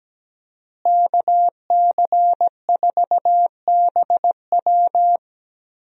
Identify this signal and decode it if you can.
Morse code, heard as a keyed tone, about 17 wpm, 700 Hz. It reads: KC4BW
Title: USA Call Signs - 5 Characters - Repeated 3x - 17wpm